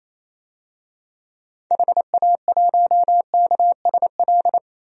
Morse code, heard as a keyed tone, about 28 wpm, 700 Hz. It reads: HA1KSL
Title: DXCC-100 Call Signs - 6 Characters - 28wpm